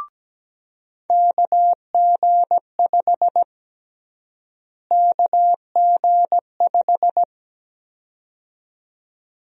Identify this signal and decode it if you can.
Morse code, heard as a keyed tone, about 17 wpm, 700 Hz. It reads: KG5 KG5
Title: USA Call Sign Prefixes - Repeated 2x - 17wpm